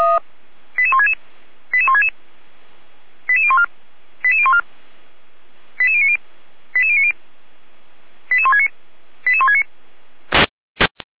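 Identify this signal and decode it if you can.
Five-tone selective call (ZVEI), heard as quick runs of five tones, tones of about 70 ms. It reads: 88166 88166 88014 88014 88090 88090 88169 88169
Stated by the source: Five-tone ZVEI selective call received by SDR, from the Feuerwehr Orlamünde fire brigade audio server.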